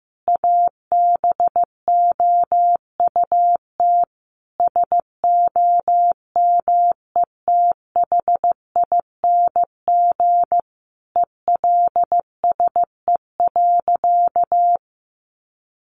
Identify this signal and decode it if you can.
Morse code, heard as a keyed tone, about 15 wpm, 700 Hz. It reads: ABOUT SOMETHING ELSE.
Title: Sentences from Top 800 Words 15wpm